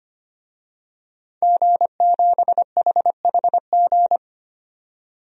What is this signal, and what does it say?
Morse code, heard as a keyed tone, about 25 wpm, 700 Hz. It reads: G7HHG
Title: DXCC-100 Call Signs - 5 Characters - 25wpm